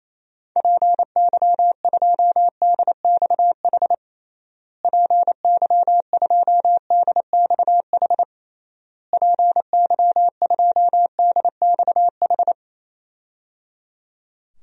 Morse code, heard as a keyed tone, about 28 wpm, 700 Hz. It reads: PY2DXH PY2DXH PY2DXH
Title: DXCC-100 Call Signs - 6 Characters - Repeated 3x - 28wpm